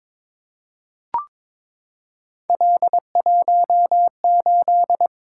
Morse code, heard as a keyed tone, about 22 wpm, 700 Hz. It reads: L18
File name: practice / 22wpm / Three Letter-Number Combinations - 22wpm